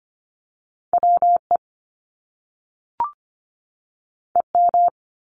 Morse code, heard as a keyed tone, about 25 wpm, 700 Hz. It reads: WE EM
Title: Top Two Letter Combinations in Words 25wpm